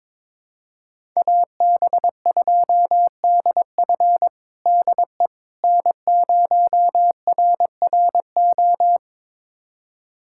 Morse code, heard as a keyed tone, about 22 wpm, 700 Hz. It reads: AB2DF DE N0RRO
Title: USA Callsign DE Callsign 22wpm